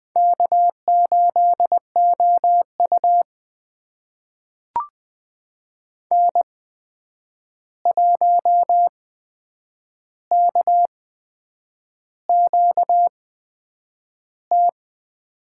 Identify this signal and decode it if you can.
Morse code, heard as a keyed tone, about 20 wpm, spaced as for 8 wpm, 700 Hz. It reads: K8OU N1KQT
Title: USA Call Signs - 8x Character Spacing - 20wpm